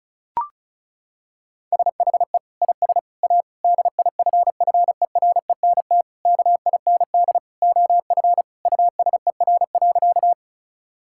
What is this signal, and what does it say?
Morse code, heard as a keyed tone, about 36 wpm, 700 Hz. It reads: SHE IS A DIFFERENT KIND OF USER.
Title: Sentences from Top 1000 Words - Review Part 5 - 35wpm